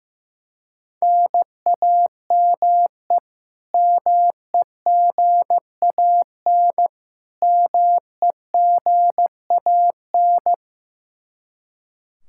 Morse code, heard as a keyed tone, about 15 wpm, 700 Hz. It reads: NAME MEGAN MEGAN
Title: Practice QSO - Exchange 1 - Beginning - 15wpm